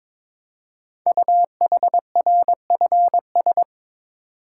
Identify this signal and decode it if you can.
Morse code, heard as a keyed tone, about 22 wpm, 700 Hz. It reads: UHRFS
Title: Code Groups - 5 Letters - 22wpm